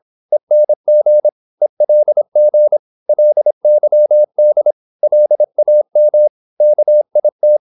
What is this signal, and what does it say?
Morse code, heard as a keyed tone, about 26 wpm, 600 Hz. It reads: ENG ELG LYD LAM KIT
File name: Korte ord DK 26wpm